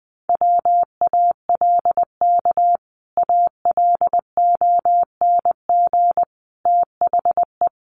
Morse code, heard as a keyed tone, about 20 wpm, 700 Hz. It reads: WALK ALONG THE
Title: Sentences from Top 1100 Words 20wpm